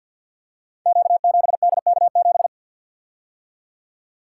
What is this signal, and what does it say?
Morse code, heard as a keyed tone, about 52 wpm, 700 Hz. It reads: Q7DK7
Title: Code Groups - 5 Letter:Numbers - 50wpm